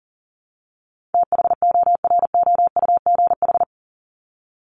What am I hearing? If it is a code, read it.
Morse code, heard as a keyed tone, about 40 wpm, 700 Hz. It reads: THOROUGH
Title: LY Suffix - Top 100 Words with Base Words 40wpm